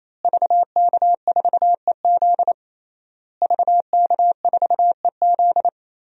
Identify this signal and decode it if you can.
Morse code, heard as a keyed tone, about 28 wpm, 700 Hz. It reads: VK4EZ VK4EZ
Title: DXCC-100 Call Signs - 5 Characters - Repeated 3x - 28wpm